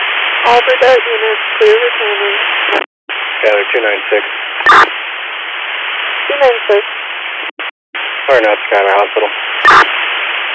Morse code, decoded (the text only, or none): none